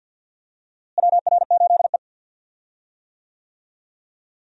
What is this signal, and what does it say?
Morse code, heard as a keyed tone, about 52 wpm, 700 Hz. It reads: WR8E
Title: USA Call Signs - 4 Characters - Repeated 3x - 50wpm